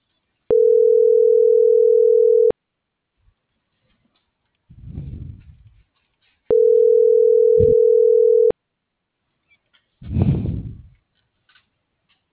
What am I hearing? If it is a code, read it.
Ringback tone (North America), 440 + 480 Hz.
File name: CENTRAL MULTIMODAL / out viridiana.juarez - .82486